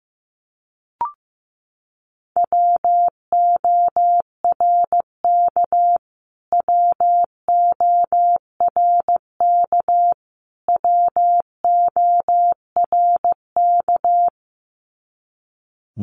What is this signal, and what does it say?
Morse code, heard as a keyed tone, about 15 wpm, 700 Hz. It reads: WORK WORK WORK
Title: Top 200 Words Repeated 15wpm